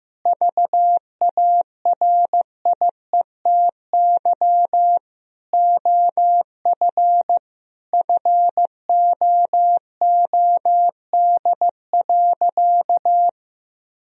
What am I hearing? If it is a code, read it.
Morse code, heard as a keyed tone, about 15 wpm, 700 Hz. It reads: VARIETY OF FOOD.
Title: Sentences from Top 1000 Words - Review Part 4 - 15wpm